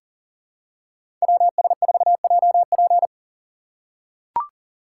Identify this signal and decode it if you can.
Morse code, heard as a keyed tone, about 40 wpm, 700 Hz. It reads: WS4JP
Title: USA Call Signs - 5 Characters - 3x Character Spacing - 40wpm